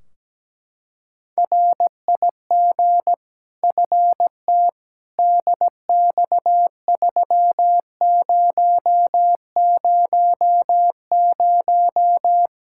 Morse code, heard as a keyed tone, about 17 wpm, 700 Hz. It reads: RIG FT DX3000
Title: Practice QSO - Exchange 2 - Info - 17wpm